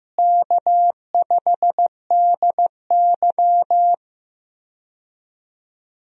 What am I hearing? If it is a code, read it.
Morse code, heard as a keyed tone, about 15 wpm, 700 Hz. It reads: K5DY
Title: USA Call Signs - 4 Characters - Repeated 3x - 15wpm